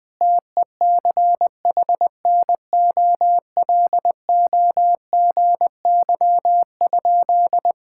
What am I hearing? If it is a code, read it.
Morse code, heard as a keyed tone, about 20 wpm, 700 Hz. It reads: TECHNOLOGY?
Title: Sentences from Top 1000 Words - Review Part 5 - 20wpm